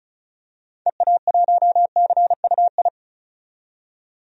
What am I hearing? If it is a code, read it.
Morse code, heard as a keyed tone, about 36 wpm, 700 Hz. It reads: EA1CUI
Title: DXCC-100 Call Signs - 6 Characters - 35wpm